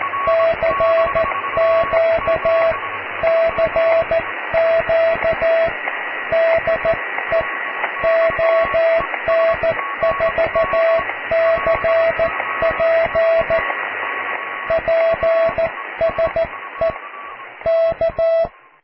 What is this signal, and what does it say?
Morse code, heard as a keyed tone, about 14 wpm, 660 Hz, with radio noise. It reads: CQ CQ DE ON4CP PSE K
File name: cw / QSO+QRM